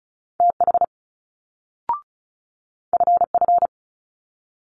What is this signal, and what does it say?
Morse code, heard as a keyed tone, about 36 wpm, 700 Hz. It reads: TH FF
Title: Multi-letter Phonemes Quick Fire 35wpm